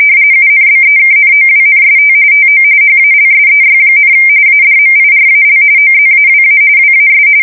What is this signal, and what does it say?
RTTY, about 45.45 baud. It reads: UNBEKNOWNST TO HALSEY, ADMIRAL OZAWA WAS
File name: fsk170 / USS-New-Jersey